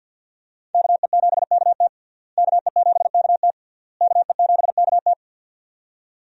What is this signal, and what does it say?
Morse code, heard as a keyed tone, about 48 wpm, 700 Hz. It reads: KE7KT KE7KT KE7KT
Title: USA Call Signs - 5 Characters - Repeated 3x - 50wpm